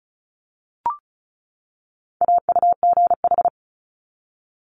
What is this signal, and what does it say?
Morse code, heard as a keyed tone, about 34 wpm, 700 Hz. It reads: AUGH
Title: Multi-letter Phonemes 35wpm